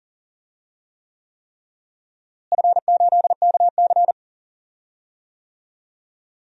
Morse code, heard as a keyed tone, about 40 wpm, 700 Hz. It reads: F8KC
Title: DXCC-100 Call Signs - 4 Characters - Repeated 3x - 40wpm